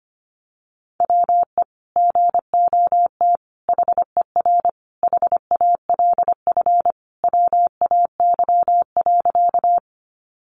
Morse code, heard as a keyed tone, about 25 wpm, 700 Hz. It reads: WE GOT HER HALF WAY.
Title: Sentences from Top 400 Words 25wpm